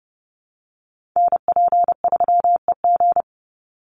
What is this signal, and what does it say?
Morse code, heard as a keyed tone, about 30 wpm, 700 Hz. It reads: NP3EG
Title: USA Call Signs 30wpm